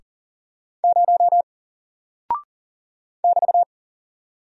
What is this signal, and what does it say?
Morse code, heard as a keyed tone, about 40 wpm, 700 Hz. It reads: 0 <BT>
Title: Single Character (Letter:Number:Punctuation:Prosign) Alt 40wpm